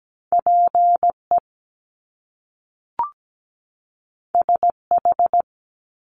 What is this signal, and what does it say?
Morse code, heard as a keyed tone, about 17 wpm, 700 Hz. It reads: PE SH
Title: Top Two Letter Combinations in Words 17wpm